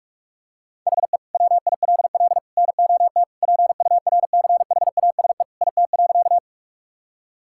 Morse code, heard as a keyed tone, about 41 wpm, 700 Hz. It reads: HE WILL NOT PURCHASE IT.
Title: Sentences from Top 1000 Words - Review Part 5 - 45wpm